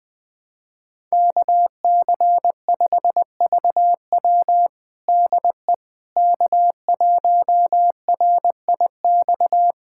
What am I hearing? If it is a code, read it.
Morse code, heard as a keyed tone, about 21 wpm, 700 Hz. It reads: KC5VW DE K1RIX
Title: USA Callsign DE Callsign 20wpm